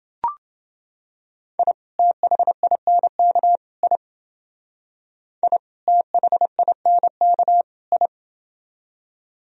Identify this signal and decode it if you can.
Morse code, heard as a keyed tone, about 30 wpm, 700 Hz. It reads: I THINK I I THINK I